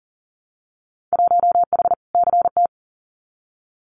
Morse code, heard as a keyed tone, about 40 wpm, 700 Hz. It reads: 1H CT
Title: Winter Field Day Exchanges 40wpm